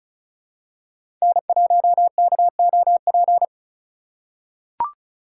Morse code, heard as a keyed tone, about 36 wpm, 700 Hz. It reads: N1KOP
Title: USA Call Signs - 4x Character Spacing - 35wpm